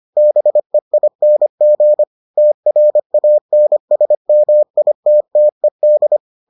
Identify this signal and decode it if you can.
Morse code, heard as a keyed tone, about 25 wpm, 600 Hz. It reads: BEING TRANSMITTED